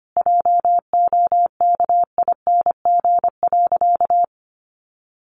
Morse code, heard as a keyed tone, about 25 wpm, 700 Hz. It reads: JOKING.
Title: A Sound Of Thunder 25wpm